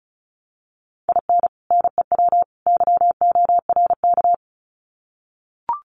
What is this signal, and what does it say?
Morse code, heard as a keyed tone, about 34 wpm, 700 Hz. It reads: IN NEW YORK
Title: Sets of 3 Words 35wpm